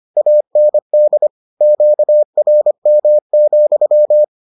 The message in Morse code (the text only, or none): AND QRM,